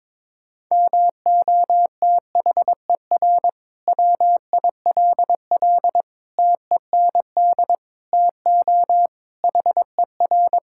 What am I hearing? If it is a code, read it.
Morse code, heard as a keyed tone, about 22 wpm, 700 Hz. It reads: MOTHER WILL TEND TO HER
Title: Sentences from Top 1000 Words - Review Part 5 - 22wpm